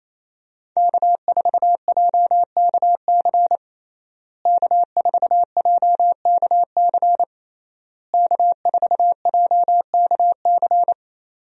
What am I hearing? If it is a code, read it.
Morse code, heard as a keyed tone, about 28 wpm, 700 Hz. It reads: K4JKC K4JKC K4JKC
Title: USA Call Signs - 5 Characters - Repeated 3x - 28wpm